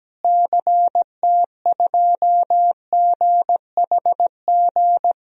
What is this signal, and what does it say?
Morse code, heard as a keyed tone, about 17 wpm, 700 Hz. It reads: CT2GHG
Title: DXCC-100 Call Signs - 6 Characters - Repeated 3x - 17wpm